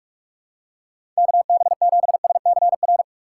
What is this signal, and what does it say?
Morse code, heard as a keyed tone, about 47 wpm, 700 Hz. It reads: KB7SCR
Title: USA Call Signs - 6 Characters - 2x Character Spacing - 45wpm